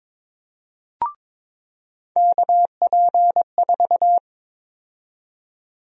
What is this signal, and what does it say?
Morse code, heard as a keyed tone, about 22 wpm, 700 Hz. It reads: KP4
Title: USA Call Sign Prefixes - 22wpm